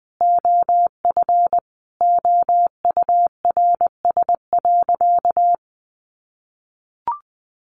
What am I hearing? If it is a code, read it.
Morse code, heard as a keyed tone, about 20 wpm, 700 Hz. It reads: OF OURS.
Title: Sentences from Top 1300 Words 20wpm